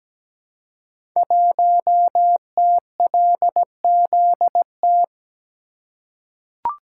Morse code, heard as a keyed tone, about 17 wpm, 700 Hz. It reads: 1TLZT
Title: Code Groups - 5 Letter:Numbers - 17wpm